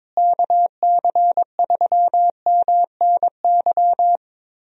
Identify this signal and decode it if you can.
Morse code, heard as a keyed tone, about 22 wpm, 700 Hz. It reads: KC3MNY
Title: USA Call Signs - 6 Characters - Repeated 2x - 22wpm